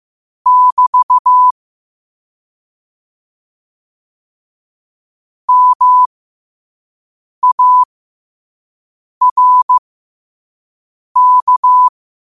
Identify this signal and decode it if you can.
Morse code, heard as a keyed tone, about 15 wpm, spaced as for 5 wpm, 1000 Hz. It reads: <BT> MARK
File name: qotdfiles / 5wpm